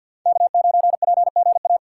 Morse code, heard as a keyed tone, about 53 wpm, 700 Hz. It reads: K9PGA
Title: USA Call Signs - 5 Characters - 8x Character Spacing - 50wpm